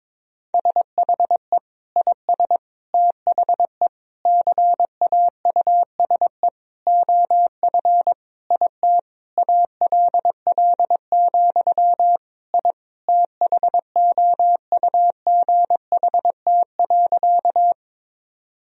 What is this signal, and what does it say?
Morse code, heard as a keyed tone, about 21 wpm, 700 Hz. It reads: SHE IS THE CAUSE OF IT ALL, I THOUGHT.
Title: Sentences from Top 1000 Words - Review Part 1 - 22wpm